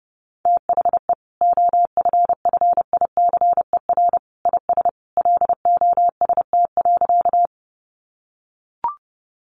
Morse code, heard as a keyed tone, about 30 wpm, 700 Hz. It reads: THE OFFICER IS LOST.
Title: Sentences from Top 1400 Words 30wpm